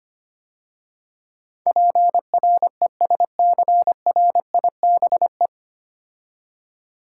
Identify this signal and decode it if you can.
Morse code, heard as a keyed tone, about 25 wpm, 700 Hz. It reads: PRESCRIBE
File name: practice / 25wpm / 9 Letter Words - First 3 Letters Given - 25wpm